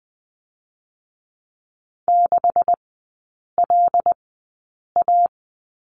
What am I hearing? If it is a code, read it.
Morse code, heard as a keyed tone, about 20 wpm, spaced as for 8 wpm, 700 Hz. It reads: 6LA
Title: 127 Lesson 21e - Number 6 - Three Characters 20wpm